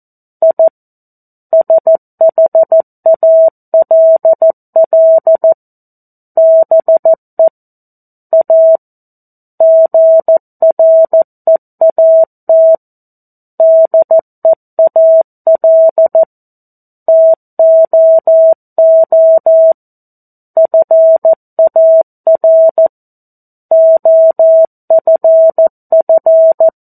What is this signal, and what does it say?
Morse code, heard as a keyed tone, about 14 wpm, 650 Hz. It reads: I SHALL BE A GREAT DEAL TOO FAR OFF
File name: 02-Chapter-02 14 WPM